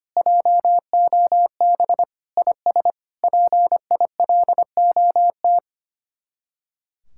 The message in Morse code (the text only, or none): JOB IS PILOT